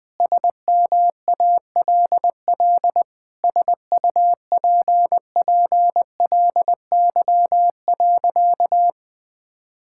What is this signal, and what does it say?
Morse code, heard as a keyed tone, about 20 wpm, 700 Hz. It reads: SMALL SUPPLY.
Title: Sentences from Top 1000 Words - Review Part 5 - 20wpm